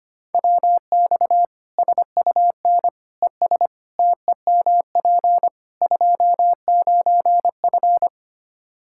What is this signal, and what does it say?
Morse code, heard as a keyed tone, about 25 wpm, 700 Hz. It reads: WX SUN ES TEMP 29F
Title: Practice QSO - Exchange 2 - Info - 25wpm